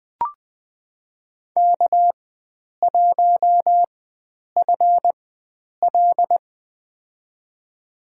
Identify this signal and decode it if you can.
Morse code, heard as a keyed tone, about 20 wpm, spaced as for 9 wpm, 700 Hz. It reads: K1FL
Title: USA Call Signs - 4x Character Spacing - 20wpm